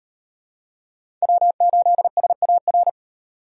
Morse code, heard as a keyed tone, about 37 wpm, 700 Hz. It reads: W8SAR